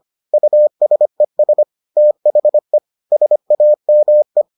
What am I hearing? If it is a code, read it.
Morse code, heard as a keyed tone, about 25 wpm, 600 Hz. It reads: USES THE SAME